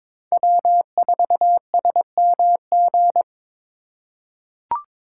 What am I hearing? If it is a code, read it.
Morse code, heard as a keyed tone, about 22 wpm, 700 Hz. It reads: W4SMG
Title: USA Call Signs - 5 Characters - 3x Character Spacing - 22wpm